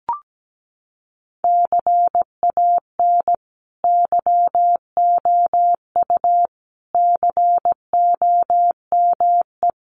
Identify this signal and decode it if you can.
Morse code, heard as a keyed tone, about 17 wpm, 700 Hz. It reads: CAN YOU COME
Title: Sentences from Top 1500 Words 17wpm